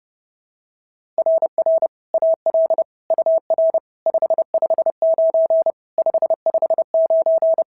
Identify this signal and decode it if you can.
Morse code, heard as a keyed tone, about 26 wpm, 650 Hz. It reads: RR AL UR 559 559
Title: SOTA QSO Practice 30wpm